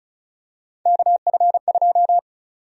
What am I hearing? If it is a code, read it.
Morse code, heard as a keyed tone, about 34 wpm, 700 Hz. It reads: KF2
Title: USA Call Sign Prefixes - Repeated 2x - 35wpm